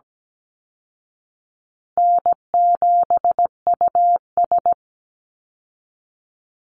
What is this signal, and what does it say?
Morse code, heard as a keyed tone, about 17 wpm, 700 Hz. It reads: N7US
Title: Call Signs Repeated 2x - 17wpm